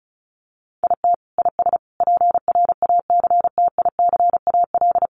Morse code, heard as a keyed tone, about 34 wpm, 700 Hz. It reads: IT IS PRACTICAL
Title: Sentences from Top 1400 Words 35wpm